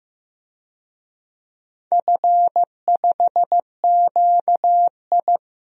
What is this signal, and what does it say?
Morse code, heard as a keyed tone, about 15 wpm, 700 Hz. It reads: F5QI
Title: DXCC-100 Call Signs - 4 Characters - 15wpm